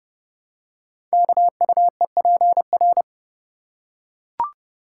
Morse code, heard as a keyed tone, about 30 wpm, 700 Hz. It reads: KUEPR
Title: Code Groups - 5 Letters - 30wpm